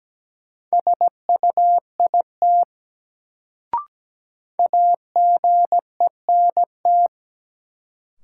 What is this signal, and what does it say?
Morse code, heard as a keyed tone, about 17 wpm, 700 Hz. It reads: SUIT AGENT
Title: Sentences from Top 1600 Words - New Word Drill - 17wpm